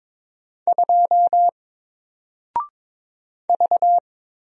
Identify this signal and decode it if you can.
Morse code, heard as a keyed tone, about 22 wpm, 700 Hz. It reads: 2 V
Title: Single Character (Letter:Number:Punctuation:Prosign) Alt 22wpm